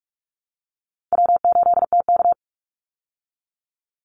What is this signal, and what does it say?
Morse code, heard as a keyed tone, about 44 wpm, 700 Hz. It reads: W8TK
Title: Call Signs Repeated 3x - 45wpm